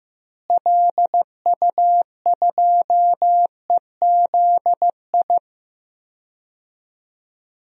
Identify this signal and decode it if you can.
Morse code, heard as a keyed tone, about 15 wpm, 700 Hz. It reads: LU2EZI
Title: DXCC-100 Call Signs - 6 Characters - Repeated 2x - 15wpm